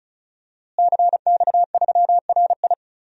Morse code, heard as a keyed tone, about 36 wpm, 700 Hz. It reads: CX3RI
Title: DXCC-100 Call Signs - 5 Characters - Repeated 3x - 35wpm